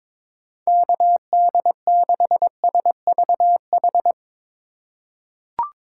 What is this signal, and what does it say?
Morse code, heard as a keyed tone, about 22 wpm, 700 Hz. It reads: KD6SVH
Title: USA Call Signs - 4x Character Spacing - 22wpm